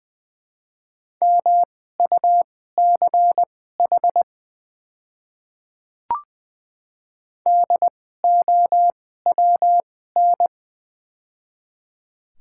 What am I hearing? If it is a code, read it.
Morse code, heard as a keyed tone, about 21 wpm, spaced as for 15 wpm, 700 Hz. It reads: MUCH DOWN
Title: 2x Character Spacing - Top 300 Words - 20wpm